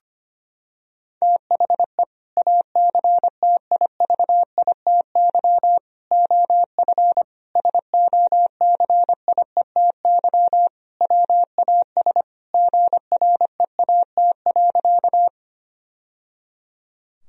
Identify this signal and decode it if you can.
Morse code, heard as a keyed tone, about 25 wpm, 700 Hz. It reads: THE ACTIVITY OF SOCIETY WAS GREAT.
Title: Sentences from Top 1000 Words - Review Part 4 - 25wpm